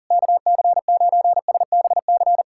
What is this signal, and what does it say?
Morse code, heard as a keyed tone, about 40 wpm, 700 Hz. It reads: KC9SBC
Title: USA Callsign DE Callsign - 6 Characters - 40wpm